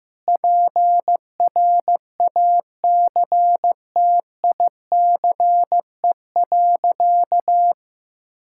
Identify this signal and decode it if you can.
Morse code, heard as a keyed tone, about 15 wpm, 700 Hz. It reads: PRACTICE.
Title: Sentences from Top 1000 Words - Review Part 6 - 15wpm